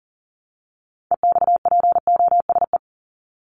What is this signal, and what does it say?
Morse code, heard as a keyed tone, about 40 wpm, 700 Hz. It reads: EXPOSE
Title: Top 1501-1750 Words 40wpm